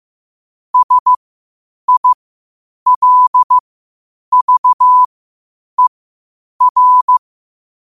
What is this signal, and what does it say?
Morse code, heard as a keyed tone, about 15 wpm, spaced as for 8 wpm, 1000 Hz. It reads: SILVER
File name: qotdfiles / 8wpm